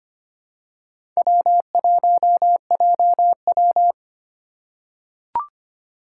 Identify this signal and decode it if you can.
Morse code, heard as a keyed tone, about 25 wpm, 700 Hz. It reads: W1JW
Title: USA Call Signs - 4 Characters - 2x Character Spacing - 25wpm